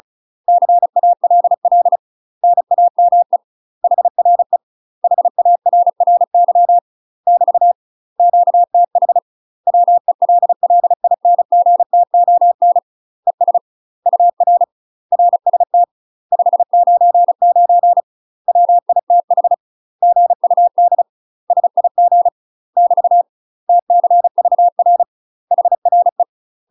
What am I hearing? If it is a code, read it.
Morse code, heard as a keyed tone, about 37 wpm, 700 Hz. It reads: CALL NAME HRE HARRY <BT> QTH WELLINGTON ES UR RST 599 WITH GUD SIG <BT> TCVR HRE
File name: UK MASTER QSOs TRAINING / 35WPM / UK MASTER QSOs 0000